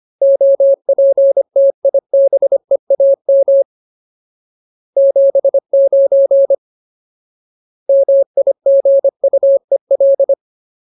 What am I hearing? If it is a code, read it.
Morse code, heard as a keyed tone, about 25 wpm, 550 Hz. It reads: OPTIBEAM 79 MIGUEL
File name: Ultimate Word List / 25WPM Ultimate